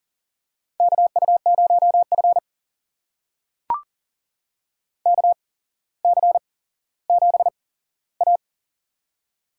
Morse code, heard as a keyed tone, about 43 wpm, spaced as for 17 wpm, 700 Hz. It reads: KU0F KC7A
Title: USA Call Signs - 4 Characters - 8x Character Spacing - 40wpm